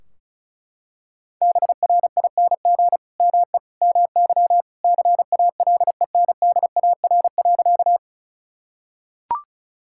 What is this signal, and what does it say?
Morse code, heard as a keyed tone, about 34 wpm, 700 Hz. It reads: BRING ME MY CALENDAR.
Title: Sentences from Top 1600 Words 35wpm